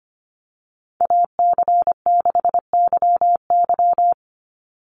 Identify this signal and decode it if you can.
Morse code, heard as a keyed tone, about 25 wpm, 700 Hz. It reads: AC6YY
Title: Call Signs - 25wpm